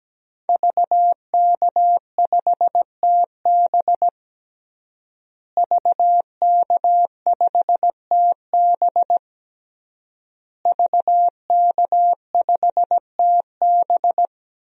Morse code, heard as a keyed tone, about 17 wpm, 700 Hz. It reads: VK5TB VK5TB VK5TB
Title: DXCC-100 Call Signs - 5 Characters - Repeated 3x - 17wpm